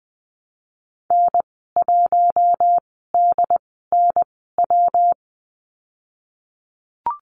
Morse code, heard as a keyed tone, about 20 wpm, spaced as for 15 wpm, 700 Hz. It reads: N1DNW
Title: 122 Lesson 20g - Number 3 - Call Signs 20wpm